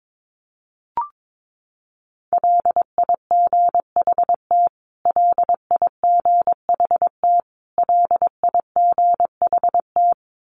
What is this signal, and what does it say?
Morse code, heard as a keyed tone, about 22 wpm, 700 Hz. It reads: LIGHT LIGHT LIGHT
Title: Top 200 Words Repeated 22wpm